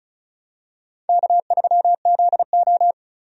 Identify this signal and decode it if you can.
Morse code, heard as a keyed tone, about 36 wpm, 700 Hz. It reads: K3ZO
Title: USA Call Signs - 4 Characters - 35wpm